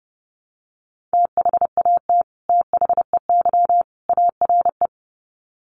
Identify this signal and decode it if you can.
Morse code, heard as a keyed tone, about 30 wpm, 700 Hz. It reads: THAT THEY ARE
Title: Sets of 3 Words 30wpm